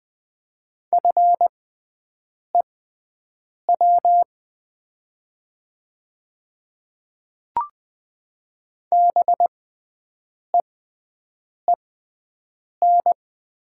Morse code, heard as a keyed tone, about 20 wpm, spaced as for 7 wpm, 700 Hz. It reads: FEW BEEN
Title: 6x Character Spacing - Top 300 Words - 20wpm